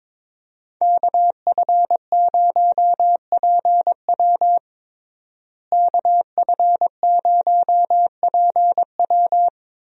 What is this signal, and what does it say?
Morse code, heard as a keyed tone, about 22 wpm, 700 Hz. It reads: KF0PW KF0PW
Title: USA Call Signs - 5 Characters - Repeated 3x - 22wpm